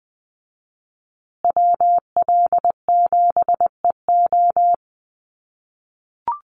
Morse code, heard as a keyed tone, about 20 wpm, 700 Hz. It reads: WL7EO
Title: USA Call Signs Repeated 3x 20wpm